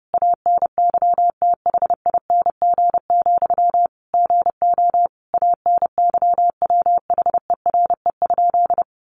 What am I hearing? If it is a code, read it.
Morse code, heard as a keyed tone, about 30 wpm, 700 Hz. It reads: ANYTHING, GO ANYWHERE?
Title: Sentences from Top 1400 Words 30wpm